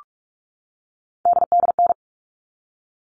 Morse code, heard as a keyed tone, about 44 wpm, 700 Hz. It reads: DDN
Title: All Cut Numbers - 3-Digit 45wpm